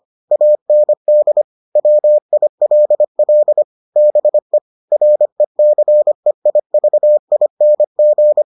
Morse code, heard as a keyed tone, about 25 wpm, 600 Hz. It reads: AND WILL BE RECEIVING